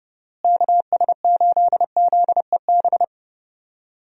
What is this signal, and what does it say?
Morse code, heard as a keyed tone, about 30 wpm, 700 Hz. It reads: KS8ZEB
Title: USA Call Signs - 6 Characters - Repeated 3x - 30wpm